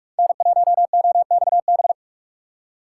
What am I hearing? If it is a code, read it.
Morse code, heard as a keyed tone, about 46 wpm, 700 Hz. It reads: N1OXB
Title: USA Call Signs - 5 Characters - 4x Character Spacing - 45wpm